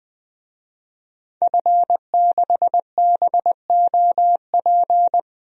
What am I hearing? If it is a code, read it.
Morse code, heard as a keyed tone, about 20 wpm, 700 Hz. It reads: F6BOP
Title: DXCC-100 Call Signs - 5 Characters - 20wpm